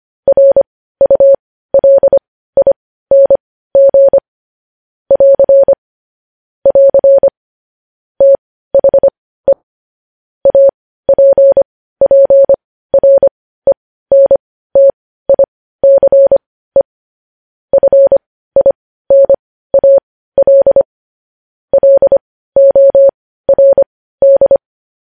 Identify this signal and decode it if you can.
Morse code, heard as a keyed tone, about 25 wpm, spaced as for 15 wpm, 550 Hz. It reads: RULING <AR> <AR> THE APPRENTICE FINAL LORD
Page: News Headlines in Morse Code at 15 WPM